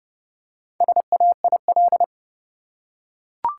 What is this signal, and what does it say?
Morse code, heard as a keyed tone, about 30 wpm, 700 Hz. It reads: SAIL
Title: Four Letter Words - 30wpm